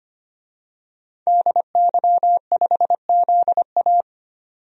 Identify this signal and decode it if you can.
Morse code, heard as a keyed tone, about 25 wpm, 700 Hz. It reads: DY5ZA
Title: DXCC-100 Call Signs - 5 Characters - 25wpm